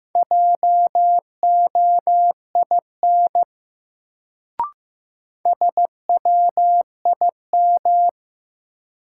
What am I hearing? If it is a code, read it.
Morse code, heard as a keyed tone, about 15 wpm, 700 Hz. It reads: JOIN SWIM
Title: Sentences from Top 1000 Words - Review Part 6 - New Word Drill - 15wpm